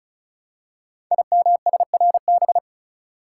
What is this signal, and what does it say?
Morse code, heard as a keyed tone, about 36 wpm, 700 Hz. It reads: IMSRB
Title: Code Groups - 5 Letters - 35wpm